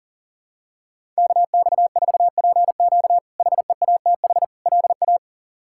Morse code, heard as a keyed tone, about 40 wpm, 700 Hz. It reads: KX4PQ HEATH LA
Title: K1USN SST QSO Practice - 40wpm